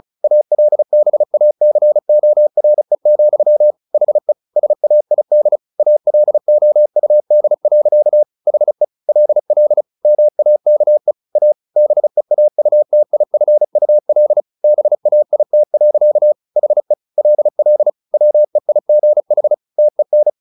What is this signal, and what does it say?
Morse code, heard as a keyed tone, about 34 wpm, 600 Hz. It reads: ALBACORE, HE SAID ALOUD. HE LL MAKE A BEAUTIFUL BAIT. HE LL WEIGH TEN
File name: the old man and the sea / the old man and the sea 1 35wpm